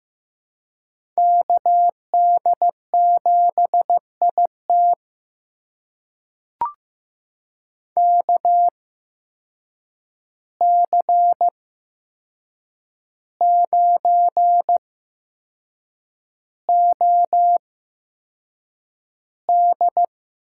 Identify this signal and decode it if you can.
Morse code, heard as a keyed tone, about 15 wpm, spaced as for 6 wpm, 700 Hz. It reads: KD7IT KC9OD
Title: USA Call Signs - 5 Characters - 8x Character Spacing - 15wpm